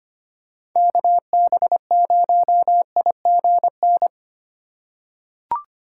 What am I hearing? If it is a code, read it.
Morse code, heard as a keyed tone, about 25 wpm, 700 Hz. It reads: KB0IGN
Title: USA Call Signs - 3x Character Spacing - 25wpm